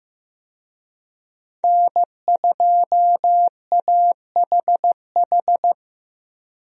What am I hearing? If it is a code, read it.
Morse code, heard as a keyed tone, about 15 wpm, 700 Hz. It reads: N2AHH
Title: USA Call Signs - 5 Characters - 15wpm